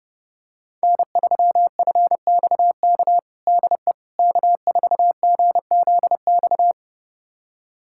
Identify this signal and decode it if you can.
Morse code, heard as a keyed tone, about 30 wpm, 700 Hz. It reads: N3FXK DE K4GZX
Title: USA Callsign DE Callsign - 5 Characters - 30wpm